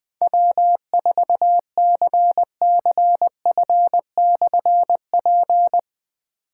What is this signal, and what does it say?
Morse code, heard as a keyed tone, about 20 wpm, 700 Hz. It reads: W4CCF/P
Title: Difficult Call Signs 20wpm